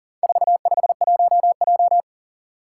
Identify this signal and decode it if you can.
Morse code, heard as a keyed tone, about 40 wpm, 700 Hz. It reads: 451J
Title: Four Challenging Letter-Number Combinations 40wpm